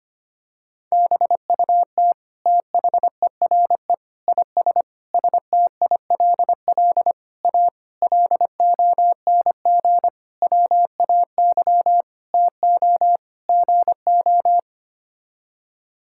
Morse code, heard as a keyed tone, about 25 wpm, 700 Hz. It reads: BUT THERE IS STILL A LONG WAY TO GO
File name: Sets of 9 Words - 25wpm